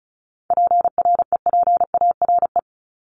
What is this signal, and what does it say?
Morse code, heard as a keyed tone, about 34 wpm, 700 Hz. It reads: PREPARE
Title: Top 751-1000 Words 35wpm